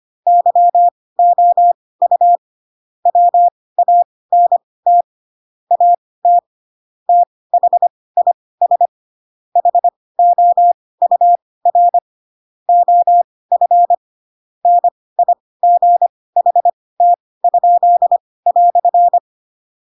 Morse code, heard as a keyed tone, about 25 wpm, spaced as for 18 wpm, 700 Hz. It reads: YOU WANT AT THIS HOUR OF NIGHT?"
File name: Pinocchio / Pinocchio ch6 12 25 ik1hge